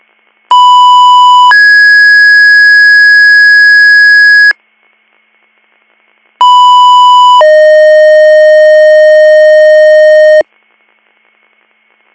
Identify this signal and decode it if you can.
Two-tone page, heard as a one-second tone, then a three-second tone, about 990 then 1650 Hz, about 990 then 620 Hz. It is heard from 0.5 to 10.4 s.